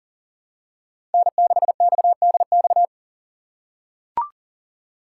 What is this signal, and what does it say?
Morse code, heard as a keyed tone, about 37 wpm, 700 Hz. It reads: N6XDX
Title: USA Call Signs - 5 Characters - 3x Character Spacing - 40wpm